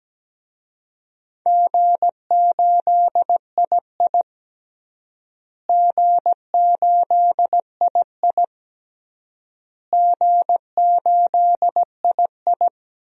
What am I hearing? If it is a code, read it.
Morse code, heard as a keyed tone, about 17 wpm, 700 Hz. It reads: G8II G8II G8II
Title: DXCC-100 Call Signs - 4 Characters - Repeated 3x - 17wpm